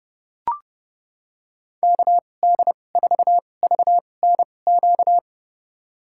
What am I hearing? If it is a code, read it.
Morse code, heard as a keyed tone, about 32 wpm, spaced as for 22 wpm, 700 Hz. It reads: KD4VNQ
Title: USA Call Signs - 2x Character Spacing - 30wpm